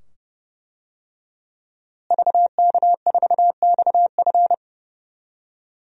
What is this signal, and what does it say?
Morse code, heard as a keyed tone, about 32 wpm, 700 Hz. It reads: VK4XF
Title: DXCC-100 Call Signs - 5 Characters - Repeated 2x - 30wpm